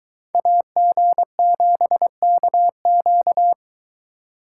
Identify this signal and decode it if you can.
Morse code, heard as a keyed tone, about 22 wpm, 700 Hz. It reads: AG7KQ